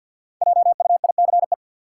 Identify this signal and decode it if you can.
Morse code, heard as a keyed tone, about 50 wpm, 700 Hz. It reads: JUICE
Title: Five Letter Words - 50wpm